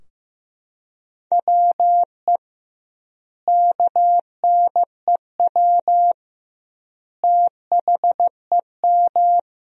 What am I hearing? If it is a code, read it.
Morse code, heard as a keyed tone, about 15 wpm, 700 Hz. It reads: WE KNEW THEM
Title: 2x Word-Spacing - Sentences from Top 400 Words - 15wpm